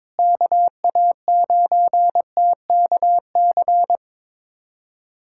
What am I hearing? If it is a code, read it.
Morse code, heard as a keyed tone, about 22 wpm, 700 Hz. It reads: KA9TKC
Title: USA Callsign DE Callsign 22wpm